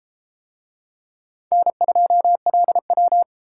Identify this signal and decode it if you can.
Morse code, heard as a keyed tone, about 29 wpm, 700 Hz. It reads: N2LW